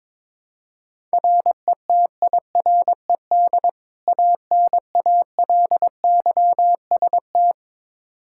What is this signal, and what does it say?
Morse code, heard as a keyed tone, about 22 wpm, 700 Hz. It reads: RETIRED ANALYST
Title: Practice QSO - Exchange 3 - More Info - 22wpm